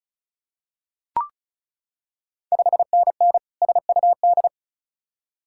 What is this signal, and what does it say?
Morse code, heard as a keyed tone, about 36 wpm, 690 Hz. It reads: 5NN SUD
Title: Ontario QSO Party Exchange 35wpm